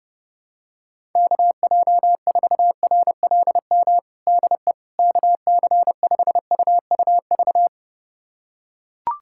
Morse code, heard as a keyed tone, about 30 wpm, 700 Hz. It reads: KJ4RLM DE KC5UUV
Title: USA Callsign DE Callsign - 6 Characters - 30wpm